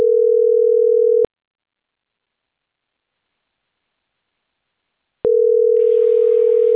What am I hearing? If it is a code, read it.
Ringback tone (North America), 440 + 480 Hz.